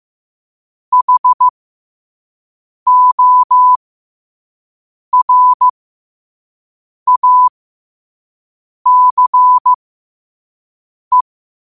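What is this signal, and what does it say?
Morse code, heard as a keyed tone, about 15 wpm, spaced as for 5 wpm, 1000 Hz. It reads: HORACE
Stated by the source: Morse code practice at 5 wpm.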